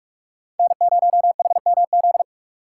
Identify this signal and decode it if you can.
Morse code, heard as a keyed tone, about 44 wpm, 700 Hz. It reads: N0HMZ